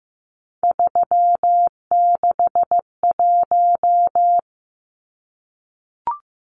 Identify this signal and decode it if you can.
Morse code, heard as a keyed tone, about 15 wpm, 700 Hz. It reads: 361
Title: Numbers - 3-Digit 15wpm